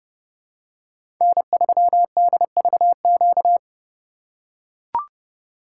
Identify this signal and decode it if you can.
Morse code, heard as a keyed tone, about 32 wpm, 700 Hz. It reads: N3DVQ
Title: USA Call Signs - 5 Characters - 4x Character Spacing - 30wpm